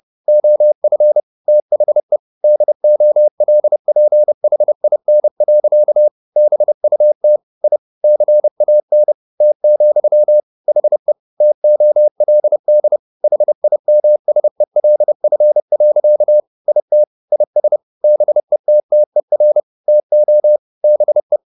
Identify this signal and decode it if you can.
Morse code, heard as a keyed tone, about 30 wpm, 600 Hz. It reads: OF THE DOLPHIN. BUT I CAN T, HE TOLD HIMSELF. IT IS BETTER TO BE